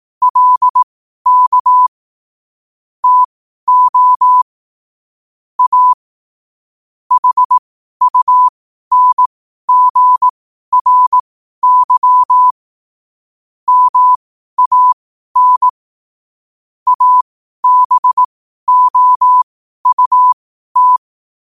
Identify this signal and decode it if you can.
Morse code, heard as a keyed tone, about 18 wpm, spaced as for 12 wpm, 1000 Hz. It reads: LK TO A HUNGRY MAN ABOUT
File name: qotdfiles / 12wpm